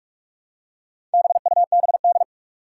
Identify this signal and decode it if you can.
Morse code, heard as a keyed tone, about 41 wpm, 700 Hz. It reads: BUBD
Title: Four Challenging Letter-Number Combinations 45wpm